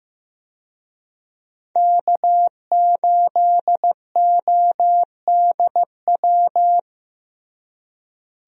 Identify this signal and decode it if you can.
Morse code, heard as a keyed tone, about 15 wpm, 700 Hz. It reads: K8ODW
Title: USA Call Signs - 5 Characters - Repeated 3x - 15wpm